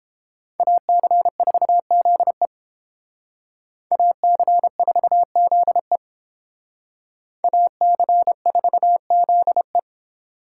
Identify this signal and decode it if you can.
Morse code, heard as a keyed tone, about 30 wpm, 700 Hz. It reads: AC4ZE AC4ZE AC4ZE